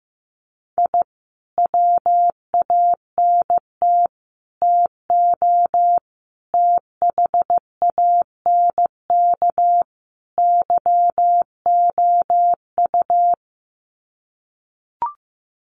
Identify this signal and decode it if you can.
Morse code, heard as a keyed tone, about 15 wpm, 700 Hz. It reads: I WANT TO THANK YOU
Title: Sets of 5 Words Encore 15wpm